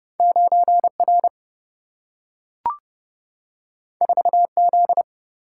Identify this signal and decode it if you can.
Morse code, heard as a keyed tone, about 30 wpm, 700 Hz. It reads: 9 R 4 Z
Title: Single-Letter Single-Number 30wpm